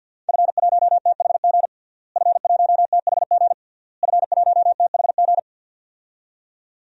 Morse code, heard as a keyed tone, about 47 wpm, 700 Hz. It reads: F1THG F1THG F1THG
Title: DXCC-100 Call Signs - 5 Characters - Repeated 3x - 50wpm